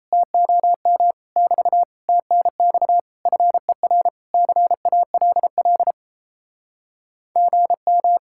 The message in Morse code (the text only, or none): TOM <BT> TNX FER CALL GM